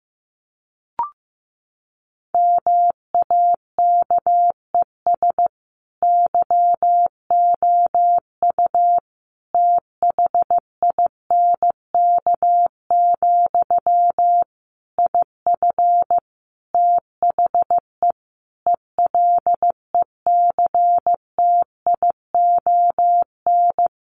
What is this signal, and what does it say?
Morse code, heard as a keyed tone, about 15 wpm, 700 Hz. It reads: MAKES YOU THINK, IF THE ELECTION
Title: A Sound Of Thunder 15wpm